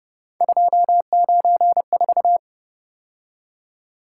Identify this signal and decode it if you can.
Morse code, heard as a keyed tone, about 31 wpm, 700 Hz. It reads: 294
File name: Speed-Racing - Three Letter-Number Combinations - 25wpm